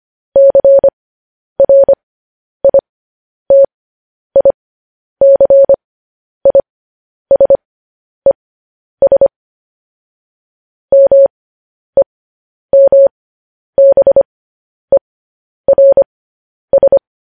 Morse code, heard as a keyed tone, about 25 wpm, spaced as for 10 wpm, 550 Hz. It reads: CRITICISES MEMBERS
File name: podcast / News headlines-1x-25x10